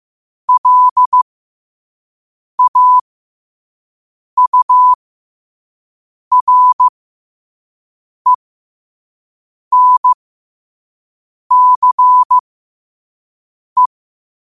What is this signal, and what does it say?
Morse code, heard as a keyed tone, about 15 wpm, spaced as for 5 wpm, 1000 Hz. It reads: LAURENCE
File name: qotdfiles / 5wpm